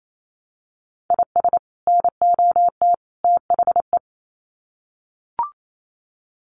Morse code, heard as a keyed tone, about 28 wpm, 700 Hz. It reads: IS NOT THE
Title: Sets of 3 Words 28wpm